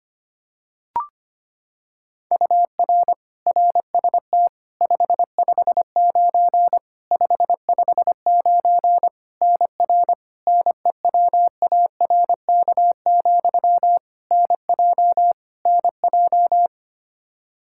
Morse code, heard as a keyed tone, about 25 wpm, 700 Hz. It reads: UR RST 559 559 NR NEWARK, NJ NJ
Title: K1USN SST QSO Practice 25wpm